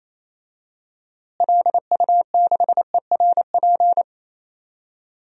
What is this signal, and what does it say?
Morse code, heard as a keyed tone, about 27 wpm, 700 Hz. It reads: LU6ERP
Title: DXCC-100 Call Signs - 6 Characters - Repeated 2x - 28wpm